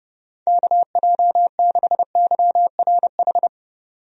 Morse code, heard as a keyed tone, about 30 wpm, 700 Hz. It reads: KJ6YRH
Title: USA Call Signs - 6 Characters - 30wpm